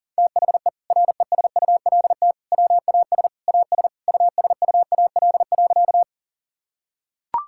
Morse code, heard as a keyed tone, about 40 wpm, 690 Hz. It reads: THE RESULT WAS AS USUAL.
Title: Sentences from Top 1000 Words - Review Part 5 - 40wpm